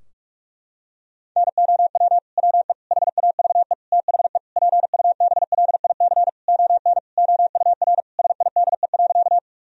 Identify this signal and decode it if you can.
Morse code, heard as a keyed tone, about 46 wpm, 700 Hz. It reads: NOW WE HAVE THE PUBLIC ON OUR SIDE.
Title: Sentences from Top 1000 Words - Review Part 3 - 45wpm